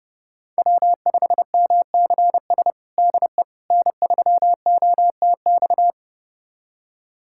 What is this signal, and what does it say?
Morse code, heard as a keyed tone, about 30 wpm, 700 Hz. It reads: W5MCS DE N3OTX
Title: USA Callsign DE Callsign - 5 Characters - 30wpm